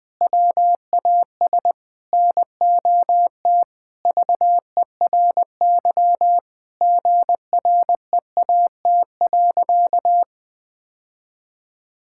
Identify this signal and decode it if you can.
Morse code, heard as a keyed tone, about 20 wpm, 700 Hz. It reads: WAS NOT VERY GREAT.
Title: Sentences from Top 1000 Words - Review Part 5 - 20wpm